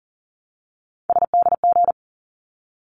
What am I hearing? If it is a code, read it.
Morse code, heard as a keyed tone, about 42 wpm, 700 Hz. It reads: SDG
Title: ARRL Sections 40wpm